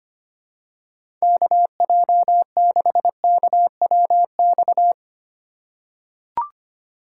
Morse code, heard as a keyed tone, about 25 wpm, 700 Hz. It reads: KJ6KWX
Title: USA Call Signs - 6 Characters - 2x Character Spacing - 25wpm